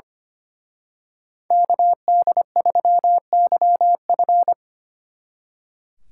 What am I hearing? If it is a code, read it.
Morse code, heard as a keyed tone, about 31 wpm, 700 Hz. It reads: KD3YF
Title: Speed Racing - USA 2x2 Call Signs 25wpm